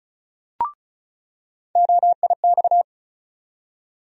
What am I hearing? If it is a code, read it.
Morse code, heard as a keyed tone, about 36 wpm, 700 Hz. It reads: OIX
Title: USA Call Sign Postfixes - 3 Letters 35wpm